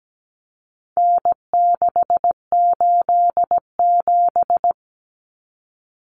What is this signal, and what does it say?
Morse code, heard as a keyed tone, about 17 wpm, 700 Hz. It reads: N687
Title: Common Cut Numbers - 4-Digit 17wpm